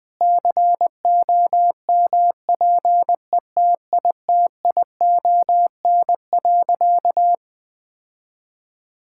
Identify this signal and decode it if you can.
Morse code, heard as a keyed tone, about 21 wpm, 700 Hz. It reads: COMPETITION.
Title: Sentences with 6-Word Phrases - Spoken Beginning - 20wpm